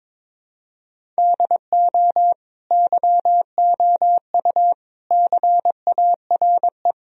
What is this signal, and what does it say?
Morse code, heard as a keyed tone, about 22 wpm, 700 Hz. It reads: DO YOU CARE
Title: Sentences from Top 1000 Words - Review Part 1 - 22wpm